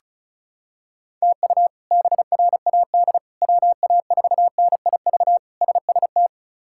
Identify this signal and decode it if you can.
Morse code, heard as a keyed tone, about 34 wpm, 700 Hz. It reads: TU BRAD WA4NIV SST
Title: K1USN SST QSO Practice - 35wpm